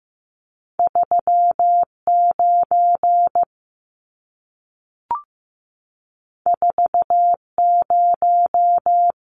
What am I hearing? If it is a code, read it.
Morse code, heard as a keyed tone, about 15 wpm, 700 Hz. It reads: 39 40
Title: Numbers - 2-Digit 15wpm